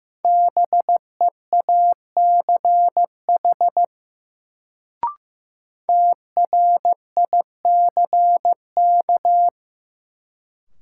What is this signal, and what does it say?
Morse code, heard as a keyed tone, about 15 wpm, 700 Hz. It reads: BEACH TRICK
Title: Sentences from Top 1500 Words - New Word Drill - 15wpm